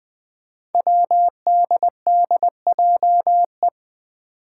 Morse code, heard as a keyed tone, about 20 wpm, 700 Hz. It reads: WDDJE
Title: Five Challenging Letter-Number Combinations 20wpm